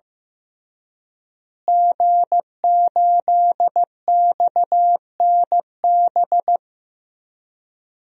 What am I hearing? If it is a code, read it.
Morse code, heard as a keyed tone, about 15 wpm, 700 Hz. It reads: G8XNB
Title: DXCC-100 Call Signs - 5 Characters - Repeated 2x - 15wpm